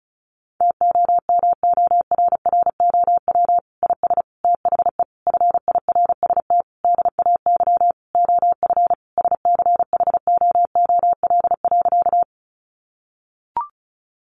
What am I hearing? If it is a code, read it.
Morse code, heard as a keyed tone, about 34 wpm, 700 Hz. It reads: TOMORROW IS THE FIRST DAY OF SCHOOL.
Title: Additional Easy Phrases 35wpm